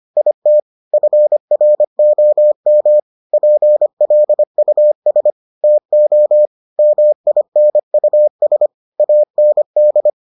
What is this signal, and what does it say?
Morse code, heard as a keyed tone, about 25 wpm, 600 Hz. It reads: IT FROM PLUS TO MINUS AND